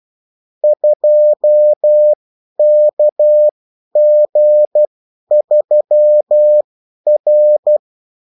Morse code, heard as a keyed tone, about 12 wpm, spaced as for 10 wpm, 600 Hz. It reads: 2KG3R
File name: morse mp3 / revision9-vitesse 10 mots